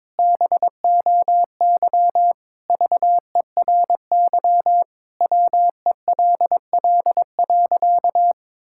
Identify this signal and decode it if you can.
Morse code, heard as a keyed tone, about 21 wpm, 700 Hz. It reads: BOY VERY WELL.
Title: Sentences from Top 1000 Words - Review Part 5 - 22wpm